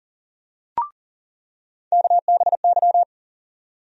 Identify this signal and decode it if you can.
Morse code, heard as a keyed tone, about 40 wpm, 700 Hz. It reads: KBY
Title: USA Call Sign Postfixes - 3 Letters 40wpm